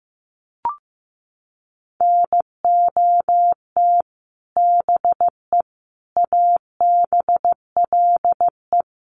Morse code, heard as a keyed tone, about 15 wpm, 700 Hz. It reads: NOT BE ABLE
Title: Sets of 4 Words 15wpm